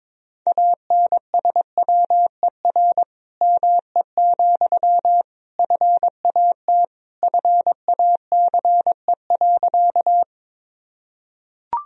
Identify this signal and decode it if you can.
Morse code, heard as a keyed tone, about 22 wpm, 700 Hz. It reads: ANSWER ME, FAT FACE.
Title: Sentences from Top 1000 Words - Review Part 4 - 22wpm